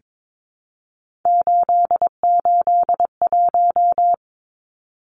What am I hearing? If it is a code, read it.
Morse code, heard as a keyed tone, about 22 wpm, 700 Hz. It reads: 881
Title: Numbers - 3-Digit 22wpm